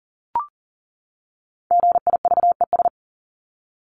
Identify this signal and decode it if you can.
Morse code, heard as a keyed tone, about 40 wpm, 700 Hz. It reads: GIVES
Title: ES:S Suffix - Top 100 Words with Base Words 40wpm